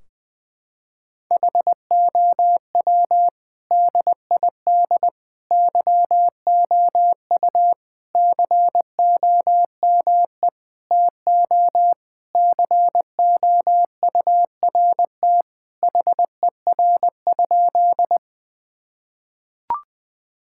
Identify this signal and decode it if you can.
Morse code, heard as a keyed tone, about 20 wpm, 700 Hz. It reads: HOW DID YOU COME TO COURT HER?
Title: Sentences from Top 1600 Words 20wpm